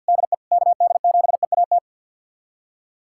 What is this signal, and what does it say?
Morse code, heard as a keyed tone, about 47 wpm, 700 Hz. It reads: DE KD7EAT
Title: USA Callsign DE Callsign - 6 Characters - 50wpm